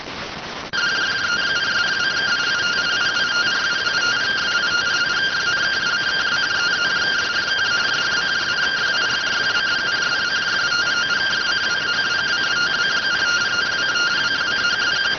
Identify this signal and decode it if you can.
Two-tone teleprinter signal, about 170 Hz shift.